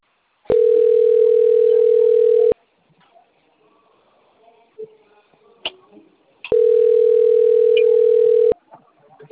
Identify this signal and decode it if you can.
Ringback tone (North America), 440 + 480 Hz.